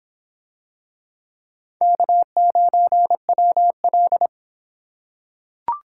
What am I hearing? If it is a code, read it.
Morse code, heard as a keyed tone, about 26 wpm, 700 Hz. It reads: K9WL